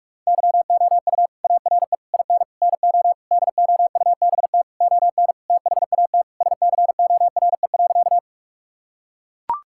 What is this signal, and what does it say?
Morse code, heard as a keyed tone, about 43 wpm, 700 Hz. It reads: YOU ARE IN NO DOUBT ON THAT SCORE.
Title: Sentences from Top 1600 Words 45wpm